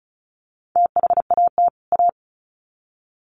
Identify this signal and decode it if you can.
Morse code, heard as a keyed tone, about 34 wpm, 700 Hz. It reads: THAT A
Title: Sets of 2 Words 35wpm